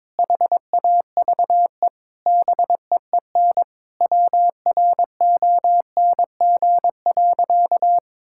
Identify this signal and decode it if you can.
Morse code, heard as a keyed tone, about 22 wpm, 700 Hz. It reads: HAVE BEEN WRONG.
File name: Sentences with 9-Word Phrases - 22wpm